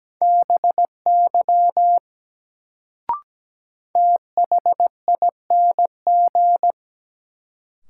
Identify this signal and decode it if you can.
Morse code, heard as a keyed tone, about 17 wpm, 700 Hz. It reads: BY THING
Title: Sentences from Top 100 Words - New Word Drill - 17wpm